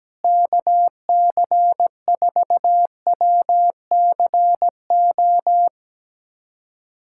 Morse code, heard as a keyed tone, about 17 wpm, 700 Hz. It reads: KC4WCO
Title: USA Callsign DE Callsign - 6 Characters - 17wpm